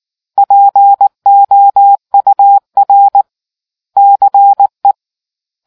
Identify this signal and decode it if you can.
Morse code, heard as a keyed tone, about 19 wpm, 800 Hz. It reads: POUR CE